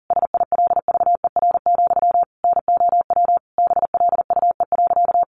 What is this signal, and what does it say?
Morse code, heard as a keyed tone, about 40 wpm, 700 Hz. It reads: SILVER, NOW BLUE.
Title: A Sound Of Thunder 40wpm